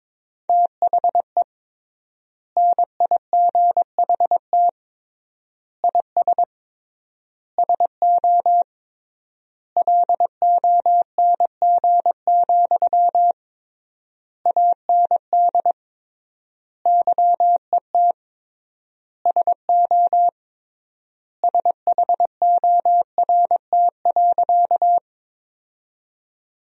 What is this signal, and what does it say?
Morse code, heard as a keyed tone, about 21 wpm, 700 Hz. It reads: THE NIGHT IS SO LONG, AND YET SO SHORT.
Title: 3x Word-Spacing - Sentences from Top 500 Words - 22wpm